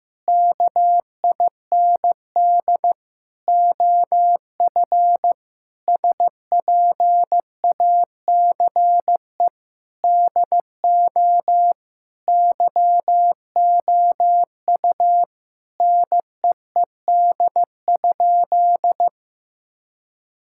Morse code, heard as a keyed tone, about 15 wpm, 700 Hz. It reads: KIND OF SPACE DO YOU NEED?
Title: Sentences from Top 1000 Words - Review Part 5 - 15wpm